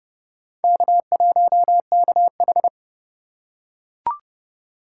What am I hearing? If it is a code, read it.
Morse code, heard as a keyed tone, about 29 wpm, 700 Hz. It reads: K1KH